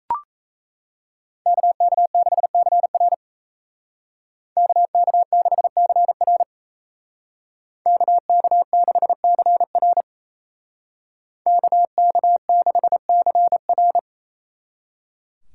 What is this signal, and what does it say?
Morse code, heard as a keyed tone, about 33 wpm, 700 Hz. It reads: KK6CR KK6CR KK6CR KK6CR
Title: Speed Racing - USA 2x2 Call Signs 28wpm